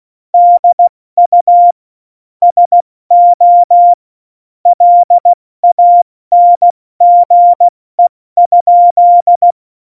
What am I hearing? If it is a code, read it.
Morse code, heard as a keyed tone, about 16 wpm, spaced as for 14 wpm, 700 Hz. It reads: DU SO LANGE?
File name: Morsekurs / Witze / Witz 29. Oktober 2019, 70 F80